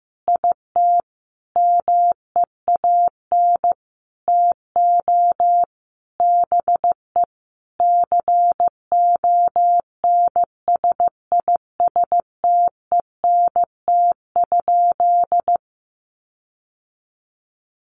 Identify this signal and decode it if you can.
Morse code, heard as a keyed tone, about 15 wpm, 700 Hz. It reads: IT MEAN TO BE CONSISTENT?
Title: Sentences from Top 1500 Words 15wpm